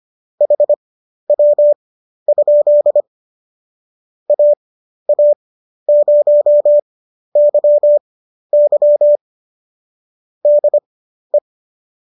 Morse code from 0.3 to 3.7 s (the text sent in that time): HW?